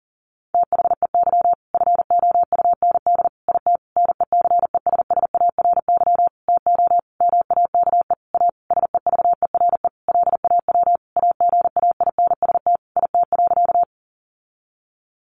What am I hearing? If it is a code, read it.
Morse code, heard as a keyed tone, about 40 wpm, 700 Hz. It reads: THEY FOUND IT NECESSARY TO MAKE A SEVERE LAW AGAINST IT.
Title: Sentences from Top 1400 Words 40wpm